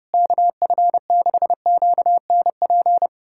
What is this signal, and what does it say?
Morse code, heard as a keyed tone, about 30 wpm, 700 Hz. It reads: KF6QNP
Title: USA Callsign DE Callsign - 6 Characters - 30wpm